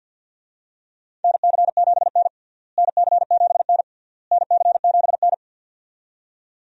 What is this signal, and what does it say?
Morse code, heard as a keyed tone, about 46 wpm, 700 Hz. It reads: NC7N NC7N NC7N
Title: USA Call Signs - 4 Characters - Repeated 3x - 50wpm